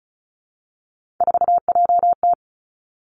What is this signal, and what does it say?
Morse code, heard as a keyed tone, about 36 wpm, 700 Hz. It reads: 4JT
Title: Three Challenging Letter-Number Combinations 35wpm